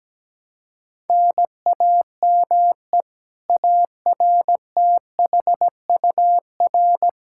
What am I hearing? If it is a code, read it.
Morse code, heard as a keyed tone, about 17 wpm, 700 Hz. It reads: NAME ARTHUR
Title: Practice QSO - Exchange 1 - Beginning - 17wpm